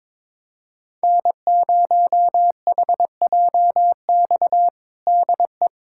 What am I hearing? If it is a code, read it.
Morse code, heard as a keyed tone, about 22 wpm, 700 Hz. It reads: N0HJX DE
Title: USA Callsign DE Callsign - 5 Characters - 22wpm